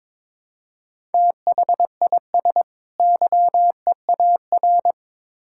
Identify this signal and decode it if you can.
Morse code, heard as a keyed tone, about 22 wpm, 700 Hz. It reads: THIS YEAR
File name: Build A Phrase - 2-Word Phrases - 22wpm